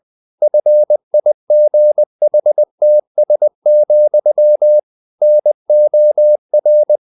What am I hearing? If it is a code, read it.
Morse code, heard as a keyed tone, about 20 wpm, 600 Hz. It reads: FIGHTS, NOR
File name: the old man and the sea / the old man and the sea part 1 20wpm